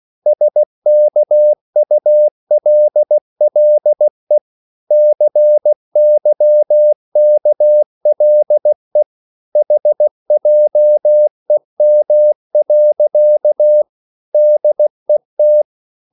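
Morse code, heard as a keyed tone, about 16 wpm, 600 Hz. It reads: SKULLE CYKLE HJEM. DET
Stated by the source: Slow (QRS) Morse code at 16 wpm, from CW med Gnister.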